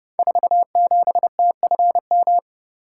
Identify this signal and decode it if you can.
Morse code, heard as a keyed tone, about 30 wpm, 700 Hz. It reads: 47TFM
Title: Code Groups - 5 Letter:Numbers - 30wpm